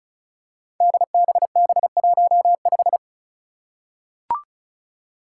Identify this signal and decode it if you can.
Morse code, heard as a keyed tone, about 34 wpm, 700 Hz. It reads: DBB15
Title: Five Challenging Letter-Number Combinations 35wpm